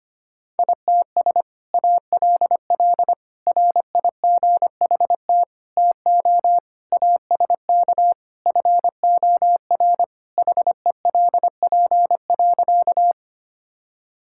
Morse code, heard as a keyed tone, about 25 wpm, 700 Hz. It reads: ITS ALL RIGHT TO ASK FOR HELP.
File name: practice / 25wpm / Sentences with 2-Word Phrases - 25wpm